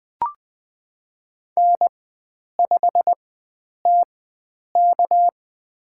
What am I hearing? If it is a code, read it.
Morse code, heard as a keyed tone, about 21 wpm, spaced as for 9 wpm, 700 Hz. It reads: N5TK
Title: USA Call Signs - 4 Characters - 4x Character Spacing - 20wpm